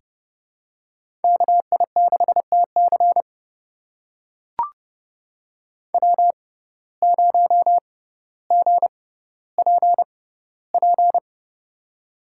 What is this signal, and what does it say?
Morse code, heard as a keyed tone, about 30 wpm, spaced as for 15 wpm, 700 Hz. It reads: KI6TC W0GPP
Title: USA Call Signs - 5 Characters - 6x Character Spacing - 30wpm